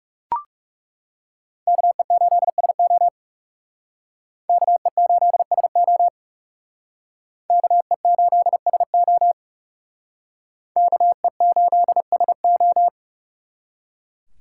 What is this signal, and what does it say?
Morse code, heard as a keyed tone, about 38 wpm, 700 Hz. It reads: KE8SO KE8SO KE8SO KE8SO